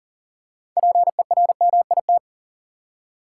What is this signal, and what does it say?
Morse code, heard as a keyed tone, about 38 wpm, 700 Hz. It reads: PERMIT
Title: Six Letter Words - 40wpm